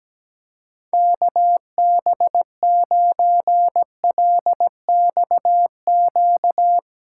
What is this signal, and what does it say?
Morse code, heard as a keyed tone, about 17 wpm, 700 Hz. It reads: KB9LXQ
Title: USA Callsign DE Callsign - 6 Characters - 17wpm